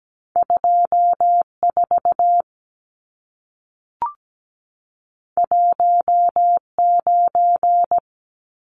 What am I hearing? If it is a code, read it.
Morse code, heard as a keyed tone, about 17 wpm, 700 Hz. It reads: 24 19
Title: Numbers - 2-Digit 17wpm